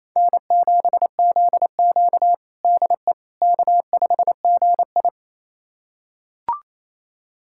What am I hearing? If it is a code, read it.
Morse code, heard as a keyed tone, about 28 wpm, 700 Hz. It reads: N7ZQ DE K5GI
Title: USA Callsign DE Callsign - 4 Characters - 28wpm